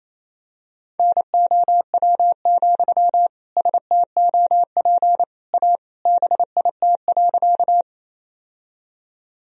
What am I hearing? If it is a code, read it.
Morse code, heard as a keyed tone, about 28 wpm, 700 Hz. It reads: NOW, STOP A BIT.
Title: Sentences from Top 1000 Words - Review Part 1 - 28wpm